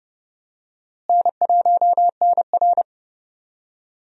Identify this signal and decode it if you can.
Morse code, heard as a keyed tone, about 30 wpm, 700 Hz. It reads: N1NR
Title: USA Call Signs - 4 Characters - Repeated 2x - 30wpm